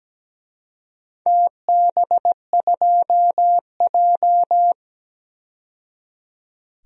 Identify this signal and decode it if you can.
Morse code, heard as a keyed tone, about 17 wpm, 700 Hz. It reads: TB2J
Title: DXCC-100 Call Signs - 4 Characters - 17wpm